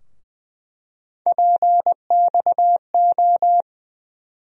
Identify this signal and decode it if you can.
Morse code, heard as a keyed tone, about 20 wpm, 700 Hz. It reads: PXO
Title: USA Call Sign Postfixes - 3 Letters 20wpm